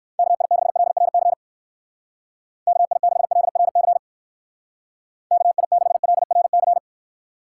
Morse code, heard as a keyed tone, about 59 wpm, 700 Hz. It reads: KI6LRC KI6LRC KI6LRC